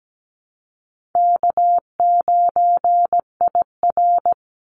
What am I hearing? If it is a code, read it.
Morse code, heard as a keyed tone, about 17 wpm, 700 Hz. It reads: K9IR
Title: Call Signs - 17wpm